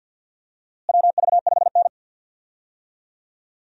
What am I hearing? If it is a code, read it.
Morse code, heard as a keyed tone, about 51 wpm, 700 Hz. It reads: WV5N
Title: USA Call Signs - 4 Characters - Repeated 3x - 50wpm